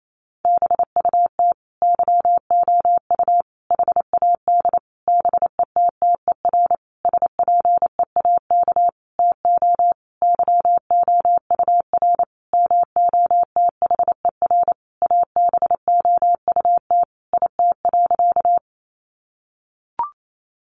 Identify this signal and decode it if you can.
Morse code, heard as a keyed tone, about 28 wpm, 700 Hz. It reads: BUT YOU HAD BETTER SPEAK TO YOUR MOTHER ABOUT IT.
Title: Sentences from Top 1100 Words 28wpm